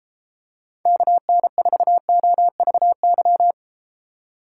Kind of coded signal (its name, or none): Morse code